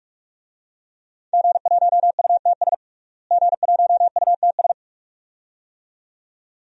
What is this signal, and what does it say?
Morse code, heard as a keyed tone, about 44 wpm, 700 Hz. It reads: G1UTS G1UTS
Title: DXCC-100 Call Signs - 5 Characters - Repeated 2x - 45wpm